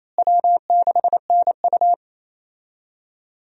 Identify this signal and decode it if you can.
Morse code, heard as a keyed tone, about 28 wpm, 700 Hz. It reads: W6NU
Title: USA Call Signs - 4 Characters - Repeated 3x - 28wpm